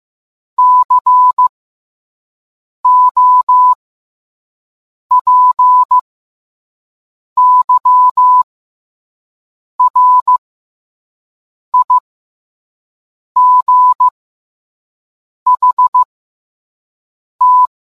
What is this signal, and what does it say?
Morse code, heard as a keyed tone, about 15 wpm, spaced as for 5 wpm, 1000 Hz. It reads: COPYRIGHT